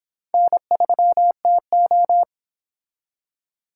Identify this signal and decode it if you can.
Morse code, heard as a keyed tone, about 29 wpm, 700 Hz. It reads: N3TO